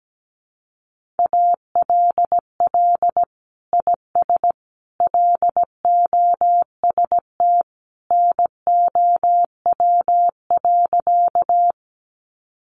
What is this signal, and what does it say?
Morse code, heard as a keyed tone, about 17 wpm, 700 Hz. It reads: ALL IS LOST NOW.
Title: Sentences from Top 1400 Words 17wpm